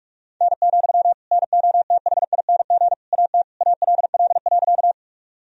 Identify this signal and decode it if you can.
Morse code, heard as a keyed tone, about 46 wpm, 700 Hz. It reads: N, NOTHING AT ALL.
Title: Sentences from Top 500 Words and Abbreviations 45wpm